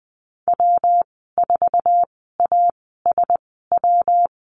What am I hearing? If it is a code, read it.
Morse code, heard as a keyed tone, about 20 wpm, spaced as for 15 wpm, 700 Hz. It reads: W4ASW
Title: 122 Lesson 20g - Number 3 - Call Signs 20wpm